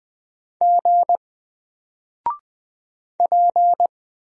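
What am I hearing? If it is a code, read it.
Morse code, heard as a keyed tone, about 20 wpm, 700 Hz. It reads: G P
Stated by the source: Single letter and number Morse drill at 20 wpm, from N4LCD.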